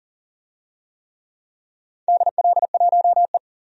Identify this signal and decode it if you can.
Morse code, heard as a keyed tone, about 40 wpm, 700 Hz. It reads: DL1E
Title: DXCC-100 Call Signs - 4 Characters - 40wpm